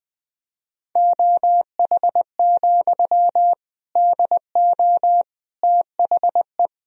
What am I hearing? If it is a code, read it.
Morse code, heard as a keyed tone, about 20 wpm, 700 Hz. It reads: OH, DO THE
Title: Sentences from Top 500 Words and Abbreviations 20wpm